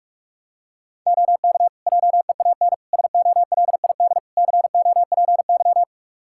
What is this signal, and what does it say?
Morse code, heard as a keyed tone, about 42 wpm, 700 Hz. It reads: OK JEAN SOLID COPY
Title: Practice QSO - Exchange 3 - More Info - 45wpm